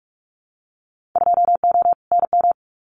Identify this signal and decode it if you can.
Morse code, heard as a keyed tone, about 44 wpm, 700 Hz. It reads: 2O NM
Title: Winter Field Day Exchanges 45wpm